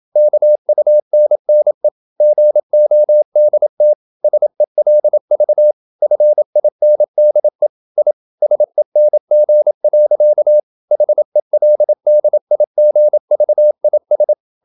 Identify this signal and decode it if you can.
Morse code, heard as a keyed tone, about 27 wpm, 600 Hz. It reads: KUNNE GODT SELV FINDE I SENG. HELDIGVIS
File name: Ferie QRQ 27wpm